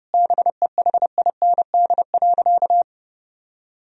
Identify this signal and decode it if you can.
Morse code, heard as a keyed tone, about 32 wpm, 700 Hz. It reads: BEHIND.
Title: Sentences from Top 1000 Words - Review Part 1 - 30wpm